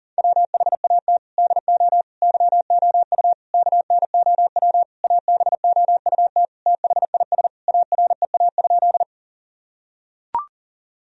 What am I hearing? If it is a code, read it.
Morse code, heard as a keyed tone, about 40 wpm, 700 Hz. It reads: WHAT DO YOU KNOW ABOUT THIS AREA?
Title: Sentences from Top 1000 Words - Review Part 2 - 40wpm